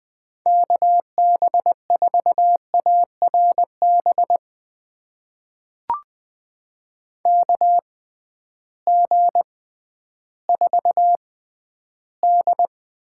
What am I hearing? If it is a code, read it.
Morse code, heard as a keyed tone, about 20 wpm, spaced as for 12 wpm, 700 Hz. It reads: KB4ARB KG4D
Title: USA Call Signs - 6 Characters - 6x Character Spacing - 20wpm